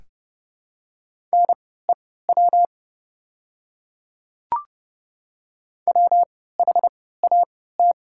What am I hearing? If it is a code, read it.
Morse code, heard as a keyed tone, about 32 wpm, spaced as for 17 wpm, 700 Hz. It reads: NEW WHAT
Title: 3x Character Spacing - Top 300 Words - 30wpm